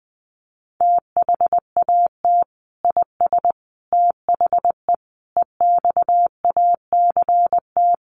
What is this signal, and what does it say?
Morse code, heard as a keyed tone, about 20 wpm, 700 Hz. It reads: THAT IS THE EXACT
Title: Sentences from Top 1300 Words 20wpm